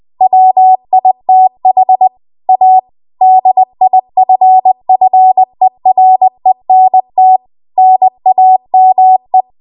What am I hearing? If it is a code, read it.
Morse code, heard as a keyed tone, about 20 wpm, 750 Hz. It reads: WITH A DIFFERENT NAME
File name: Archive / 20 WPM / 20WPM